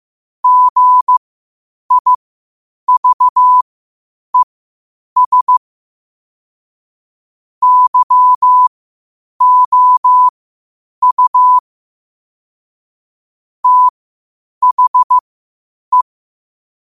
Morse code, heard as a keyed tone, about 15 wpm, spaced as for 8 wpm, 1000 Hz. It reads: GIVES YOU THE
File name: qotdfiles / 8wpm